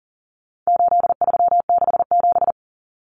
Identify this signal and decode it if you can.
Morse code, heard as a keyed tone, about 40 wpm, 700 Hz. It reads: 8367
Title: Common Cut Numbers - 4-Digit 40wpm